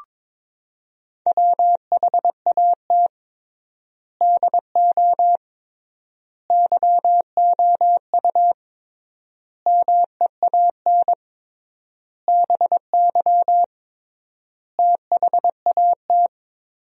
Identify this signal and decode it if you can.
Morse code, heard as a keyed tone, about 21 wpm, 700 Hz. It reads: WHAT DO YOU MEAN BY THAT
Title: 3x Word-Spacing - Sentences from Top 400 Words - 22wpm